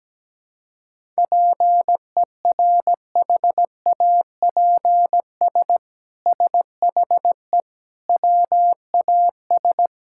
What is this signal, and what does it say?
Morse code, heard as a keyed tone, about 17 wpm, 700 Hz. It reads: PERHAPS SHE WAS
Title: Sentences from Top 1000 Words - Review Part 5 - 17wpm